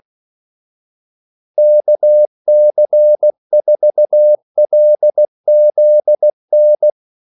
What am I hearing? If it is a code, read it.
Morse code, heard as a keyed tone, about 16 wpm, 600 Hz. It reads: KC4LZN
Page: Kaldesignaler 16 wpm